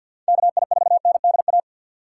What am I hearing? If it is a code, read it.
Morse code, heard as a keyed tone, about 47 wpm, 700 Hz. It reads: KI4NDA
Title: USA Call Signs - 6 Characters - 3x Character Spacing - 50wpm